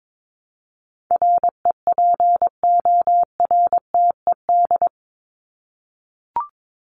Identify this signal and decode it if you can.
Morse code, heard as a keyed tone, about 22 wpm, 700 Hz. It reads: REPORTED
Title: ED Suffix - Top 100 Words with Base Words 22wpm